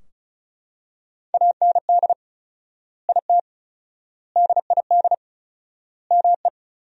Morse code, heard as a keyed tone, about 34 wpm, 700 Hz. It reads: AND IT DID ME
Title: 4x Word-Spacing - Sentences from Top 200 Words - 35wpm